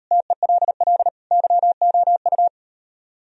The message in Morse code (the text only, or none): TELL YOU